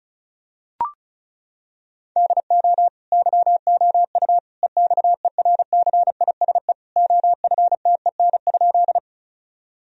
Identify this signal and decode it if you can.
Morse code, heard as a keyed tone, about 36 wpm, 700 Hz. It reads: DO YOU EXERCISE OFTEN?
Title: Sentences from Top 1000 Words - Review Part 4 - 35wpm